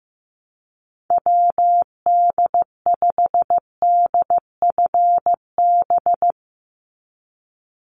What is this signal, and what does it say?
Morse code, heard as a keyed tone, about 15 wpm, 700 Hz. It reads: WD5DFB
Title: USA Call Signs 15wpm